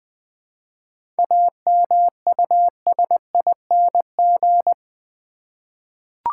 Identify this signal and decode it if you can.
Morse code, heard as a keyed tone, about 20 wpm, 700 Hz. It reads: AMUSING
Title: 7 Letter Words - First 3 Letters Given - 20wpm